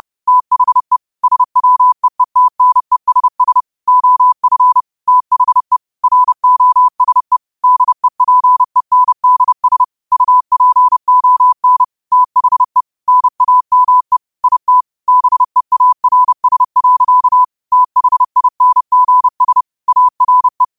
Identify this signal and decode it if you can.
Morse code, heard as a keyed tone, about 30 wpm, 1000 Hz. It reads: THE SWEETNESS OF THE ROSE DEPENDS UPON THE NAME IT BEARS. THINGS ARE